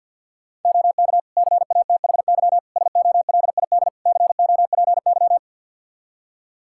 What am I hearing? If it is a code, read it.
Morse code, heard as a keyed tone, about 45 wpm, 700 Hz. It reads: OK CATHY SOLID COPY
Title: K1USN SST QSO Practice 50wpm